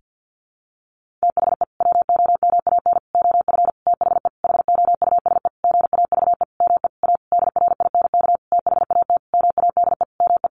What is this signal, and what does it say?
Morse code, heard as a keyed tone, about 50 wpm, 700 Hz. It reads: THE WOMAN OF THE HOUSE GAVE ME A DRINK THAT MADE ME
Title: Sentences from Top 1500 Words 50wpm